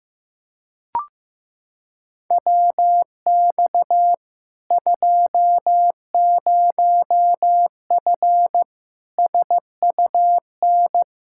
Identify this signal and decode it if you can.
Morse code, heard as a keyed tone, about 15 wpm, 700 Hz. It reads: WX 20F SUN
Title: Practice QSO - Exchange 2 - Info - 15wpm